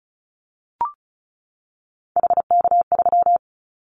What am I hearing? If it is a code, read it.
Morse code, heard as a keyed tone, about 34 wpm, 700 Hz. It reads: HK3
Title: Callsign Prefixes for Top 100 DXCC - 35wpm